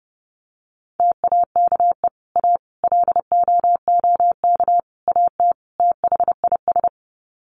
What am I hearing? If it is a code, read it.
Morse code, heard as a keyed tone, about 30 wpm, 700 Hz. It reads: TAKE A LOOK AT THIS
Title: Sets of 5 Words 30wpm